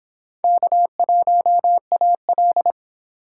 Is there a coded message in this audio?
Yes: Morse code from 0.4 to 2.7 s, heard as a keyed tone, about 26 wpm, 700 Hz. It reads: K1AL